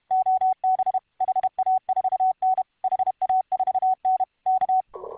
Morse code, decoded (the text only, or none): OB HA4N HA4N K